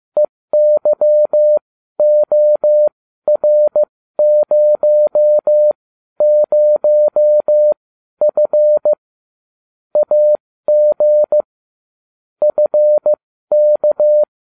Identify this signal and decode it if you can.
Morse code, heard as a keyed tone, about 15 wpm, spaced as for 12 wpm, 600 Hz. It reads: EYOR00F AG FK